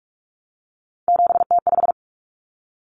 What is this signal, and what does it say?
Morse code, heard as a keyed tone, about 44 wpm, 700 Hz. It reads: 7T5
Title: Common Cut Numbers - 3-Digit 45wpm